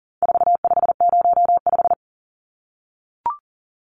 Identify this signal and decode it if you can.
Morse code, heard as a keyed tone, about 40 wpm, 700 Hz. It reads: 4505
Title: Numbers - 4-Digit 40wpm